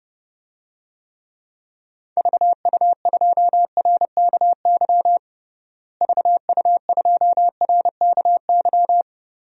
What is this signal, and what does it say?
Morse code, heard as a keyed tone, about 32 wpm, 700 Hz. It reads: VU2RKY VU2RKY
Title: DXCC-100 Call Signs - 6 Characters - Repeated 3x - 30wpm